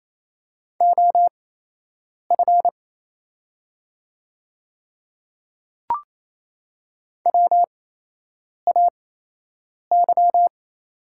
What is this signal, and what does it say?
Morse code, heard as a keyed tone, about 27 wpm, spaced as for 8 wpm, 700 Hz. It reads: OF WAY
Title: 8x Character Spacing - Top 100 Words - 28wpm